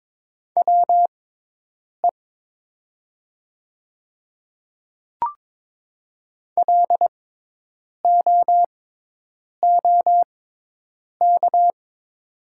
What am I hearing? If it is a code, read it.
Morse code, heard as a keyed tone, about 22 wpm, spaced as for 8 wpm, 700 Hz. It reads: WE LOOK
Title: 6x Character Spacing - Top 500 Words - 22wpm